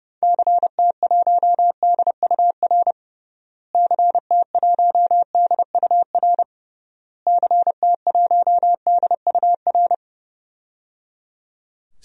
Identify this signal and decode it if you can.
Morse code, heard as a keyed tone, about 30 wpm, 700 Hz. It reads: CT1DUR CT1DUR CT1DUR
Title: DXCC-100 Call Signs - 6 Characters - Repeated 3x - 30wpm